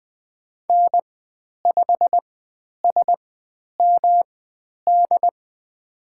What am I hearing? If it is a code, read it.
Morse code, heard as a keyed tone, about 20 wpm, spaced as for 10 wpm, 700 Hz. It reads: N5SMD
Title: 109 Lesson 18g - Letter m - Call Signs 20wpm